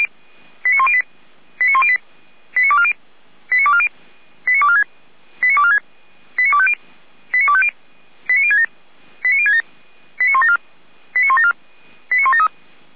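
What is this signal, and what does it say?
Five-tone selective call (ZVEI), heard as quick runs of five tones, tones of about 70 ms. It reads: 89108 89108 89240 89240 89246 89246 89250 89250 89067 89067 89174 89174 89173
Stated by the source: Five-tone ZVEI selective call 89337 received by SDR, from the Feuerwehr Orlamünde fire brigade server.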